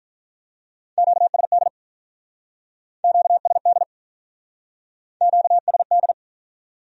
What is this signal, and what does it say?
Morse code, heard as a keyed tone, about 44 wpm, 700 Hz. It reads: QSD QSD QSD